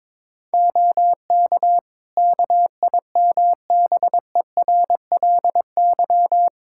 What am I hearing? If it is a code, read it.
Morse code, heard as a keyed tone, about 22 wpm, 700 Hz. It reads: OK KIMBERLY
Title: Practice QSO - Exchange 2 - Info - 22wpm